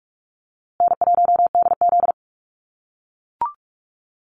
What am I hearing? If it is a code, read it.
Morse code, heard as a keyed tone, about 44 wpm, 700 Hz. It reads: N1DZ
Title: USA Call Signs Repeated 2x 45wpm